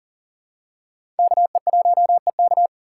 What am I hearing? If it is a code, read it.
Morse code, heard as a keyed tone, about 40 wpm, 700 Hz. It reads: KE1EK
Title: USA Call Signs - 5 Characters - 6x Character Spacing - 40wpm